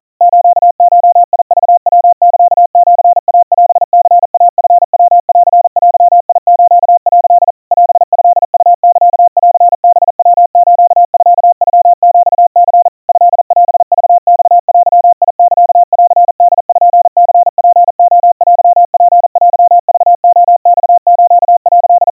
Morse code (Japanese Wabun code), heard as a keyed tone, about 40 wpm, 700 Hz. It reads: スゴクヤサシイオニイチヤンデス」 オトウサンハヤスミノヒニ トオクマデサンポニツレテツテクレマス」
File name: kawa / Morse sample / kamyu 150 cpm 700 Hz Crispy